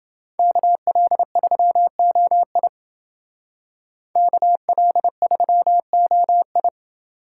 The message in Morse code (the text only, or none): KL3OI KL3OI